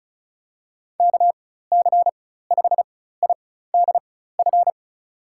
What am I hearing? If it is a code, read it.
Morse code, heard as a keyed tone, about 36 wpm, spaced as for 16 wpm, 700 Hz. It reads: KC5IDF
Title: USA Call Signs - 6 Characters - 4x Character Spacing - 35wpm